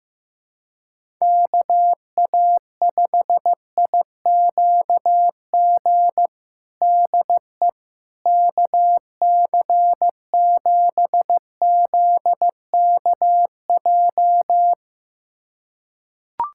Morse code, heard as a keyed tone, about 15 wpm, 700 Hz. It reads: KA5IQG DE KC7ZKJ
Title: USA Callsign DE Callsign - 6 Characters - 15wpm